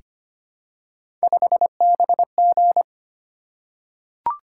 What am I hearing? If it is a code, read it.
Morse code, heard as a keyed tone, about 25 wpm, 700 Hz. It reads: 5BG
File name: Three Letter-Number Combinations - 25wpm